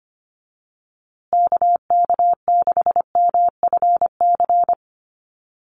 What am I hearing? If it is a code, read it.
Morse code, heard as a keyed tone, about 25 wpm, 700 Hz. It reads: KK6MFC
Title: USA Call Signs Repeated 2x 25wpm